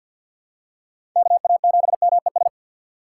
Morse code, heard as a keyed tone, about 48 wpm, 700 Hz. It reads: KA7MES
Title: USA Call Signs - 6 Characters - 3x Character Spacing - 50wpm